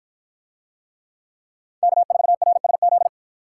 Morse code, heard as a keyed tone, about 53 wpm, 700 Hz. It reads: K4RSZ